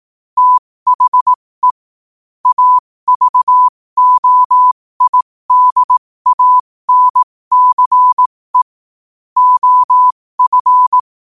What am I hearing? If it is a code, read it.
Morse code, heard as a keyed tone, about 18 wpm, spaced as for 15 wpm, 1000 Hz. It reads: THE AVOIDANCE OF
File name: qotdfiles / 15wpm